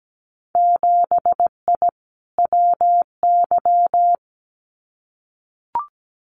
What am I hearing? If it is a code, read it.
Morse code, heard as a keyed tone, about 17 wpm, 700 Hz. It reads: 7I WY
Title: Winter Field Day Exchanges Encore 17wpm